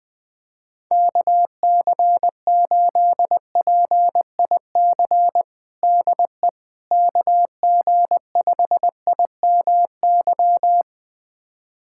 Morse code, heard as a keyed tone, about 21 wpm, 700 Hz. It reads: KC8PIC DE KG5IMY
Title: USA Callsign DE Callsign - 6 Characters - 20wpm